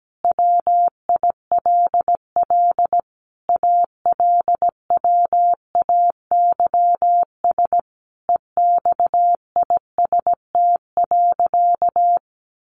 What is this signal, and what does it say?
Morse code, heard as a keyed tone, about 17 wpm, 700 Hz. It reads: WILL ALWAYS EXIST.
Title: Sentences from Top 1500 Words 17wpm